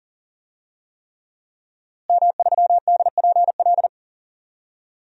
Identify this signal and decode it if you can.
Morse code, heard as a keyed tone, about 40 wpm, 700 Hz. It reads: M3DPL
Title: DXCC-100 Call Signs - 5 Characters - Repeated 3x - 40wpm